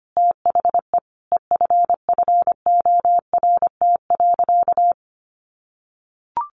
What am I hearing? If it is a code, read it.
Morse code, heard as a keyed tone, about 25 wpm, 700 Hz. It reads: THE EFFORT.
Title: Sentences from Top 1200 Words 25wpm